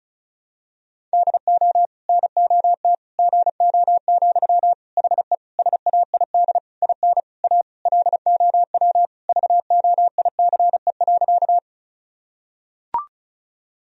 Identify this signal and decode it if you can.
Morse code, heard as a keyed tone, about 36 wpm, 700 Hz. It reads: DO NOT GO, HE SAID IN A LOW VOICE.
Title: Sentences from Top 1000 Words - Review Part 1 - 35wpm